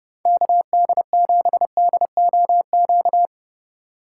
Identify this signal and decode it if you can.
Morse code, heard as a keyed tone, about 36 wpm, 700 Hz. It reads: KD7DOQ
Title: Speed Racing - USA 2x3 Call Signs 30wpm